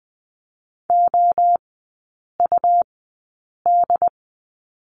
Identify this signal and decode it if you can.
Morse code, heard as a keyed tone, about 20 wpm, spaced as for 8 wpm, 700 Hz. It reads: OUD
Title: 100 Lesson 17e - Letter u - Three Characters 20wpm